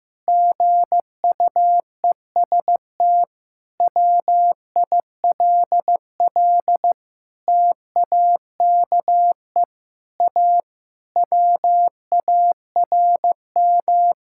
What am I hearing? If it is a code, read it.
Morse code, heard as a keyed tone, about 15 wpm, 700 Hz. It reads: GUEST WILL TAKE A WARM
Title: Sentences from Top 1600 Words 15wpm